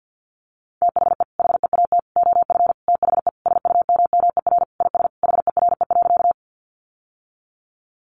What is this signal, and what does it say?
Morse code, heard as a keyed tone, about 50 wpm, 700 Hz. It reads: THE HEAT OF THE SUMMER IS HERE.
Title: Sentences from Top 1100 Words 50wpm